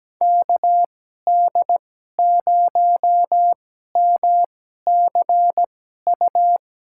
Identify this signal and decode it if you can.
Morse code, heard as a keyed tone, about 17 wpm, spaced as for 12 wpm, 700 Hz. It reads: KD0MCU
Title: USA Call Signs - 2x Character Spacing - 17wpm